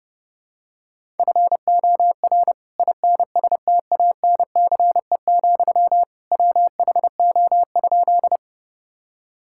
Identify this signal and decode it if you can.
Morse code, heard as a keyed tone, about 30 wpm, 700 Hz. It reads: FOR INSTANCE, WHO?
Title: Sentences from Top 1000 Words - Review Part 5 - 30wpm